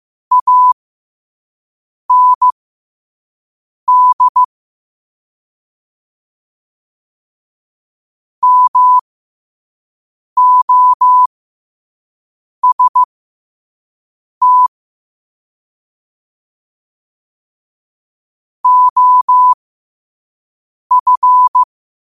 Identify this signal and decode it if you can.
Morse code, heard as a keyed tone, about 15 wpm, spaced as for 5 wpm, 1000 Hz. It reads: AND MOST OF
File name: qotdfiles / 5wpm